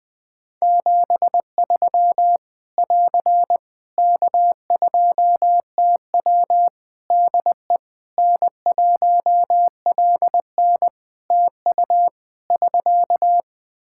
Morse code, heard as a keyed tone, about 20 wpm, 700 Hz. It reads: 73 <AR> K2TW DE N1LN TU <SK>
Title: Practice QSO - Exchange 4 - Ending - 20wpm